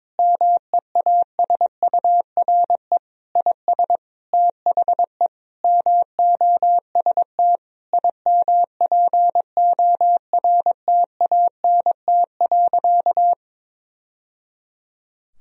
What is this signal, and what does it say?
Morse code, heard as a keyed tone, about 22 wpm, 700 Hz. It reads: MEASURE IS THE MOST IMPORTANT.
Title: Sentences from Top 1000 Words - Review Part 2 - 22wpm